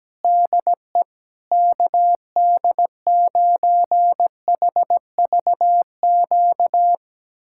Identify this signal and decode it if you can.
Morse code, heard as a keyed tone, about 17 wpm, 700 Hz. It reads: DE KD9HVQ
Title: USA Callsign DE Callsign - 6 Characters - 17wpm